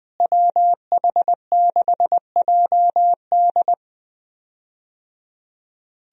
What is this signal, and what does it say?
Morse code, heard as a keyed tone, about 21 wpm, 700 Hz. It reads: WH6JD
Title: USA Call Signs - 5 Characters - Repeated 3x - 20wpm